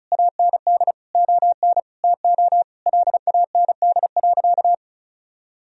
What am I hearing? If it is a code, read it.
Morse code, heard as a keyed tone, about 34 wpm, 700 Hz. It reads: AND ON TO LAND.
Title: Sentences from Top 1000 Words - Review Part 2 - 35wpm